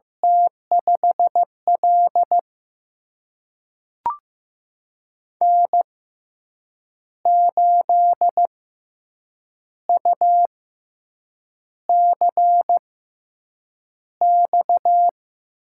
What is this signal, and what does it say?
Morse code, heard as a keyed tone, about 15 wpm, spaced as for 7 wpm, 700 Hz. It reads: T5L N8UCX
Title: USA Call Signs - 6x Character Spacing - 15wpm